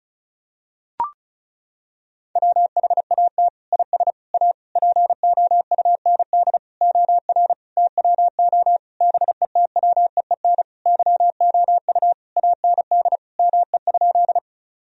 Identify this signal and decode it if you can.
Morse code, heard as a keyed tone, about 36 wpm, 700 Hz. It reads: WHAT IS A POUND OR TWO BETWEEN YOU AND ME?
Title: Sentences from Top 1000 Words - Review Part 2 - 35wpm